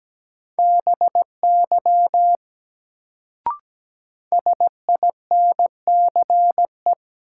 Morse code, heard as a keyed tone, about 17 wpm, 700 Hz. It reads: BY SINCE
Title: Sets of 2 Words - New Word Drill - 17wpm